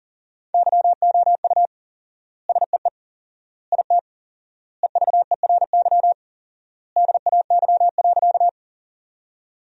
Morse code, heard as a keyed tone, about 37 wpm, 700 Hz. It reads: YOU SEE IT EVERY DAY.
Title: 4x Word-Spacing - Sentences from Top 200 Words - 40wpm